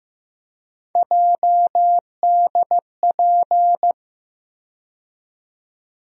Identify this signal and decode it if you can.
Morse code, heard as a keyed tone, about 16 wpm, 700 Hz. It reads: JDP
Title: USA Call Sign Postfixes - 3 Letters 15wpm